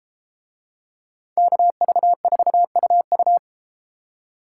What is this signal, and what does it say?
Morse code, heard as a keyed tone, about 32 wpm, 700 Hz. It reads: KV4UU